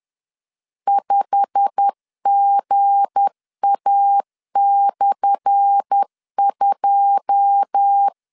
Morse code, heard as a keyed tone, about 10 wpm, 800 Hz. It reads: 5GA/2
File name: LEZIONI / 10 SIGNS / 10A SLASH 10wpm 2808